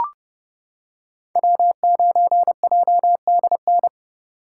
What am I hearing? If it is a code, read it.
Morse code, heard as a keyed tone, about 29 wpm, 700 Hz. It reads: W9JDN